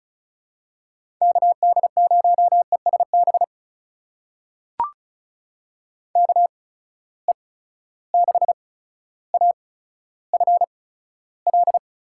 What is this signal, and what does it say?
Morse code, heard as a keyed tone, about 35 wpm, spaced as for 15 wpm, 700 Hz. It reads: KD0ESB KE6AFL
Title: USA Call Signs - 6 Characters - 8x Character Spacing - 35wpm